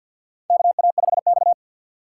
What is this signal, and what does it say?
Morse code, heard as a keyed tone, about 53 wpm, 700 Hz. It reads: KA5X
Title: USA Call Signs - 6x Character Spacing - 50wpm